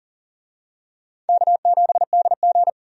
Morse code, heard as a keyed tone, about 39 wpm, 700 Hz. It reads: K7DG
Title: USA Call Signs - 4 Characters - 40wpm